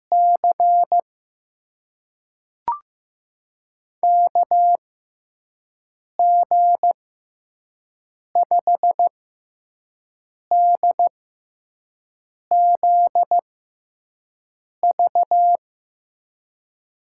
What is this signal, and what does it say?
Morse code, heard as a keyed tone, about 15 wpm, spaced as for 5 wpm, 700 Hz. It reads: C KG5DZV
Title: USA Call Signs - 6 Characters - 6x Character Spacing - 15wpm